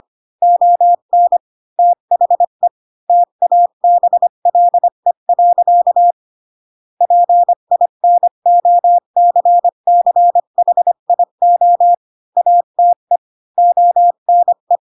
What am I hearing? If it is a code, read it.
Morse code, heard as a keyed tone, about 25 wpm, 700 Hz. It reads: ON THE TABLE. PINOCCHIO ATE ONE
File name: Pinocchio / Pinocchio ch7 20 25 ik1hge